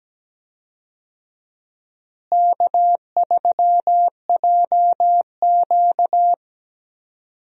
Morse code, heard as a keyed tone, about 17 wpm, 700 Hz. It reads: K3JQ
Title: USA Call Signs - 4 Characters - Repeated 3x - 17wpm